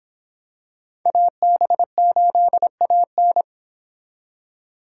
Morse code, heard as a keyed tone, about 23 wpm, 700 Hz. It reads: AB8AN